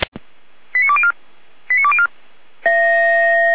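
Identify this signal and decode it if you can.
Five-tone selective call (ZVEI), heard as quick runs of five tones, tones of about 70 ms. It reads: 89204 89204
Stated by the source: ZVEI five-tone selective call 89200, received by SDR.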